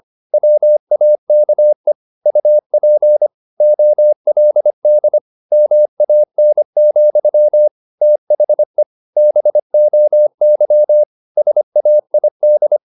Morse code, heard as a keyed tone, about 25 wpm, 600 Hz. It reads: WAKE UP OLD MAN, THE BOY SAID
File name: the old man and the sea part 1 25wpm